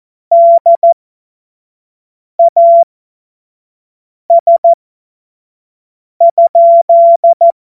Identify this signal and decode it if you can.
Morse code, heard as a keyed tone, about 14 wpm, spaced as for 5 wpm, 680 Hz. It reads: DAS?
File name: Morsekurs / Witze / Witz 21. Dezember 2021, 25 F70